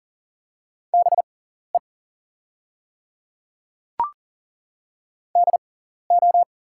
Morse code, heard as a keyed tone, about 39 wpm, spaced as for 14 wpm, 690 Hz. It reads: BE DO
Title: 6x Character Spacing - Top 100 Words - 40wpm